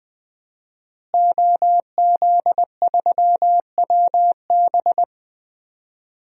Morse code, heard as a keyed tone, about 20 wpm, 700 Hz. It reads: OZ3WB
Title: DXCC-100 Call Signs - 5 Characters - Repeated 3x - 20wpm